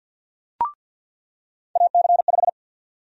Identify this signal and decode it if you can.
Morse code, heard as a keyed tone, about 47 wpm, 700 Hz. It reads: AC5
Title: USA Call Sign Prefixes - 50wpm